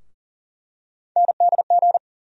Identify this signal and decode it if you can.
Morse code, heard as a keyed tone, about 40 wpm, 700 Hz. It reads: NDG
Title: USA Call Sign Postfixes - 3 Letters 40wpm